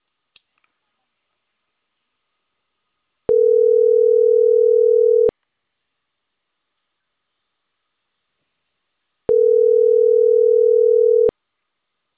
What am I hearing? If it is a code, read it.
Ringback tone (North America), 440 + 480 Hz.